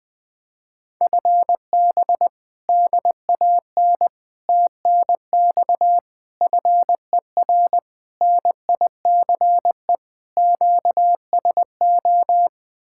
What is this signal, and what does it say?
Morse code, heard as a keyed tone, about 20 wpm, 700 Hz. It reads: FB DAN TNX FER NICE QSO
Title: Speed-Racing Common QSO Phrases - 17wpm